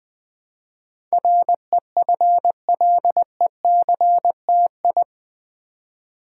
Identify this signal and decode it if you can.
Morse code, heard as a keyed tone, about 20 wpm, 700 Hz. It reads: REFLECTI
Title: Build A Word - 4 Letters At A Time - 10-Letter Words - 20wpm